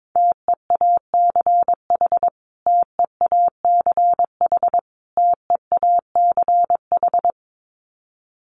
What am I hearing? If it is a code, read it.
Morse code, heard as a keyed tone, about 22 wpm, 700 Hz. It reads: TEACH TEACH TEACH
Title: Top 500 Words Repeated 22wpm